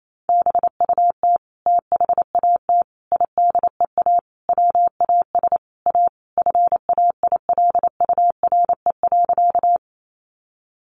Morse code, heard as a keyed tone, about 28 wpm, 700 Hz. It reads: BUT THAT IDEA WAS A FAILURE.
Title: Sentences from Top 900 Words 28wpm